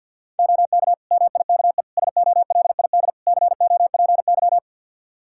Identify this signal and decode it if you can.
Morse code, heard as a keyed tone, about 50 wpm, 700 Hz. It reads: OK MIKE SOLID COPY
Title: Practice QSO - Exchange 3 - More Info - 50wpm